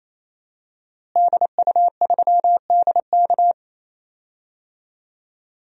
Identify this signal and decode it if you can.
Morse code, heard as a keyed tone, about 28 wpm, 690 Hz. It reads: DU3DK
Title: DXCC-100 Call Signs - 5 Characters - Repeated 3x - 28wpm